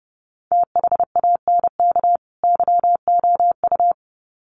Morse code, heard as a keyed tone, about 30 wpm, 700 Hz. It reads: THANK YOU
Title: Sets of 2 Words 30wpm